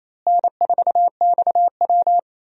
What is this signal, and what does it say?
Morse code, heard as a keyed tone, about 28 wpm, 700 Hz. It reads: N4XW
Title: USA Call Signs - 4 Characters - Repeated 2x - 28wpm